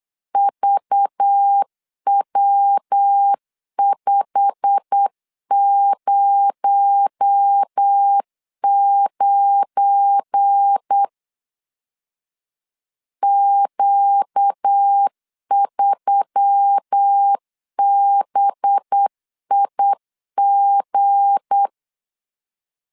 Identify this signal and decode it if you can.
Morse code, heard as a keyed tone, about 8 wpm, 800 Hz. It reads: VW509 Q3BIG
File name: LEZIONI / 8 YJ45 / 8 YJ45 8wpm 3214